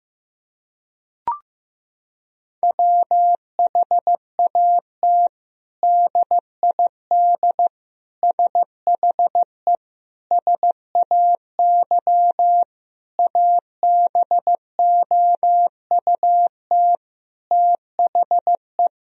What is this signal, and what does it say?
Morse code, heard as a keyed tone, about 15 wpm, 700 Hz. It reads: WHAT DID SHE SAY ABOUT THE
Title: Sentences from Top 1000 Words - Review Part 2 - 15wpm